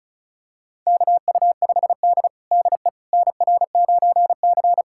Morse code, heard as a keyed tone, about 36 wpm, 700 Hz. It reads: KU5D DE NR9C
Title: USA Callsign DE Callsign - 4 Characters - 35wpm